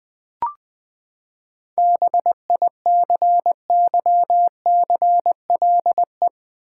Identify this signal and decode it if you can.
Morse code, heard as a keyed tone, about 20 wpm, 700 Hz. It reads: BICYCLE
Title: Seven Letter Words - 20wpm